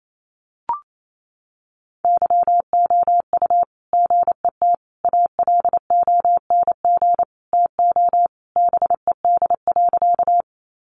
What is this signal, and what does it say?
Morse code, heard as a keyed tone, about 28 wpm, 700 Hz. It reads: YOU GET ALONG TO BED.
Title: Sentences from Top 700 Words 28wpm